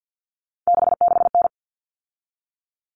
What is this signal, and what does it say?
Morse code, heard as a keyed tone, about 48 wpm, 700 Hz. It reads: 66N
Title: All Cut Numbers - 3-Digit 50wpm